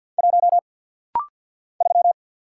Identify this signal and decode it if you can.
Morse code, heard as a keyed tone, about 50 wpm, 700 Hz. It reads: 1 3
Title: Single Letter-Number Alt - Rapid-Fire 50wpm